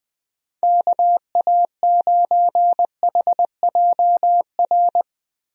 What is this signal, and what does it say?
Morse code, heard as a keyed tone, about 20 wpm, 700 Hz. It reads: KA9HJR
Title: USA Call Signs - 6 Characters - 20wpm